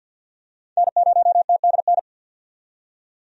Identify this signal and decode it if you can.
Morse code, heard as a keyed tone, about 49 wpm, 700 Hz. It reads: N0TDN
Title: USA Call Signs - 5 Characters - 50wpm